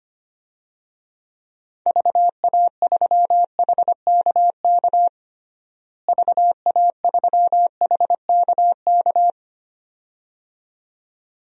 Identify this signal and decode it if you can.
Morse code, heard as a keyed tone, about 25 wpm, 700 Hz. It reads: VA3HKK VA3HKK
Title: DXCC-100 Call Signs - 6 Characters - Repeated 2x - 25wpm